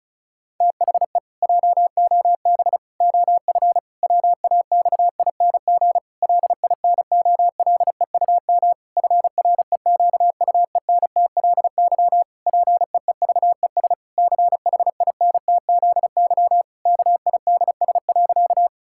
Morse code, heard as a keyed tone, about 34 wpm, 700 Hz. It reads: THE JOB OF WAXING LINOLEUM FREQUENTLY PEEVES CHINTZY KIDS.
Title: Pangram Challenge v2 - 35wpm